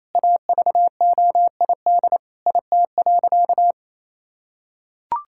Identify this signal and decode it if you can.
Morse code, heard as a keyed tone, about 27 wpm, 700 Hz. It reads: AVOID IT.
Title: Sentences from Top 1000 Words - Review Part 6 - 28wpm